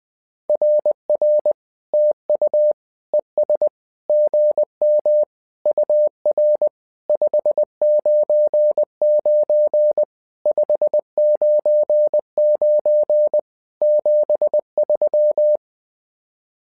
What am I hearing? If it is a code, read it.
Morse code, heard as a keyed tone, about 20 wpm, 600 Hz. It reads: RR TU ES GM UR 599 599 73
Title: SOTA QSO Practice 20wpm